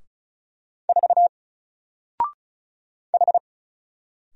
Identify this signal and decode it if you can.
Morse code, heard as a keyed tone, about 36 wpm, 700 Hz. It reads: 4 H
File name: Single-Letter-Number-35wpm